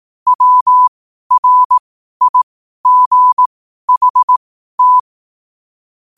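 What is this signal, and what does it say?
Morse code, heard as a keyed tone, about 18 wpm, spaced as for 12 wpm, 1000 Hz. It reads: WRIGHT
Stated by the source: Morse code at 12 wpm.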